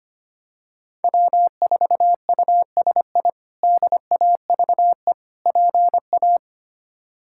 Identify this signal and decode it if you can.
Morse code, heard as a keyed tone, about 25 wpm, 700 Hz. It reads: W4USI DAVE PA
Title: K1USN SST QSO Practice - 25wpm